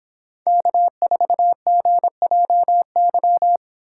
Morse code, heard as a keyed tone, about 26 wpm, 700 Hz. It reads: K4GJY